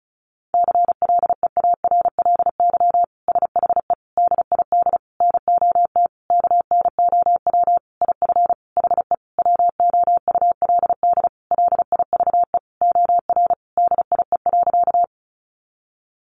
Morse code, heard as a keyed tone, about 34 wpm, 700 Hz. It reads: CLEARLY SHE DID NOT KNOW IF HE WOULD LIVE OR DIE.
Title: Sentences from Top 1300 Words 35wpm